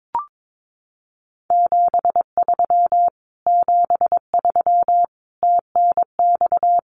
Morse code, heard as a keyed tone, about 22 wpm, 700 Hz. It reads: 73 73 TNX
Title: Common QSO Phrases Encore 22wpm